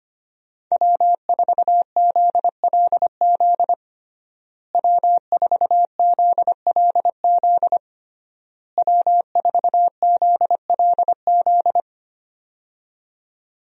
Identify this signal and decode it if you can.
Morse code, heard as a keyed tone, about 25 wpm, 700 Hz. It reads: W4ZLZ W4ZLZ W4ZLZ
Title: USA Call Signs - 5 Characters - Repeated 3x - 25wpm